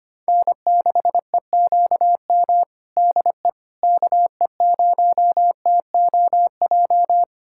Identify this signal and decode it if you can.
Morse code, heard as a keyed tone, about 25 wpm, 700 Hz. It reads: N6EQM DE KE0TOJ
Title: USA Callsign DE Callsign 25wpm